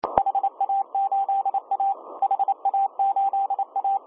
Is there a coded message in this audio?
Yes: Morse code from 0.2 to 4.0 s, heard as a keyed tone, about 28 wpm, 790 Hz, with radio noise. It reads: HA8A HA8A